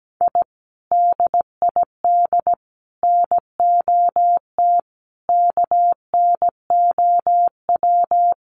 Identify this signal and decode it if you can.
Morse code, heard as a keyed tone, about 17 wpm, 700 Hz. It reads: I DID NOT KNOW
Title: Sets of 5 Words Encore 17wpm